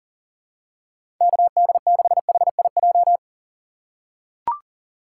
Morse code, heard as a keyed tone, about 37 wpm, 700 Hz. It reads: KD6HIJ
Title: USA Call Signs - 3x Character Spacing - 40wpm